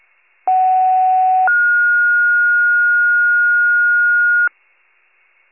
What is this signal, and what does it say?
Two-tone page, heard as a one-second tone, then a three-second tone, about 750 then 1430 Hz. It is heard from 0.5 to 4.5 s.